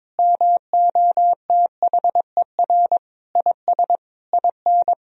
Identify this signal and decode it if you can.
Morse code, heard as a keyed tone, about 22 wpm, 700 Hz. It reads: MOTHER IS IN
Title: Sentences from Top 1000 Words - Review Part 2 - 22wpm